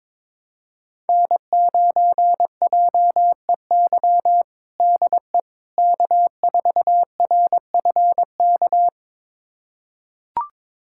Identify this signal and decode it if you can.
Morse code, heard as a keyed tone, about 22 wpm, 700 Hz. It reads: N9JEY DE K4RFK
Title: USA Callsign DE Callsign - 5 Characters - 22wpm